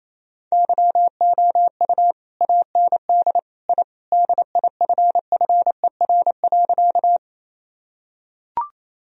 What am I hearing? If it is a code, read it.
Morse code, heard as a keyed tone, about 28 wpm, 700 Hz. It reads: YOU AND I DIFFER.
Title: Sentences from Top 1000 Words - Review Part 1 - 28wpm